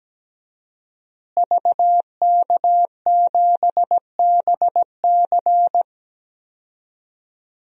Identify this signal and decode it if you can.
Morse code, heard as a keyed tone, about 17 wpm, 700 Hz. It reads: VK7BC
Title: DXCC-100 Call Signs - 5 Characters - Repeated 3x - 17wpm